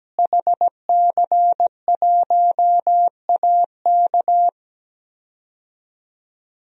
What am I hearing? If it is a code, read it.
Morse code, heard as a keyed tone, about 17 wpm, 700 Hz. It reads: HC1AK
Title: DXCC-100 Call Signs - 5 Characters - Repeated 2x - 17wpm